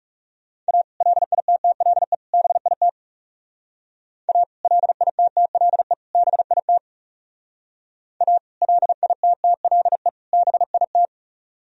Morse code, heard as a keyed tone, about 36 wpm, 700 Hz. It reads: A LITTLE BIT A LITTLE BIT A LITTLE BIT
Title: Speed-Racing - Sets of 3 Words 30wpm